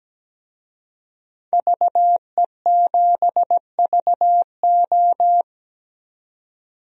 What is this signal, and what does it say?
Morse code, heard as a keyed tone, about 17 wpm, 700 Hz. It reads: VE7VO
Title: DXCC-100 Call Signs - 5 Characters - 17wpm